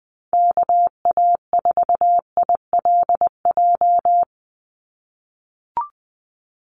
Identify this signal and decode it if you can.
Morse code, heard as a keyed tone, about 20 wpm, 700 Hz. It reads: KA4ILJ
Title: USA Call Signs 20wpm